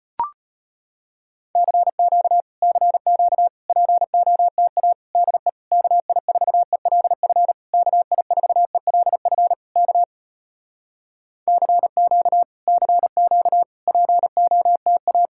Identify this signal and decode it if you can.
Morse code, heard as a keyed tone, about 35 wpm, 700 Hz. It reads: CQ CQ POTA DE KI4ELF KI4ELF K CQ CQ POTA